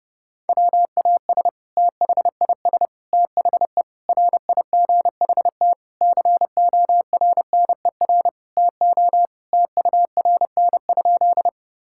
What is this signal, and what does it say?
Morse code, heard as a keyed tone, about 30 wpm, 700 Hz. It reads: WAS THIS THE RIGHT CORNER TO TURN?
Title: Sentences from Top 1600 Words 30wpm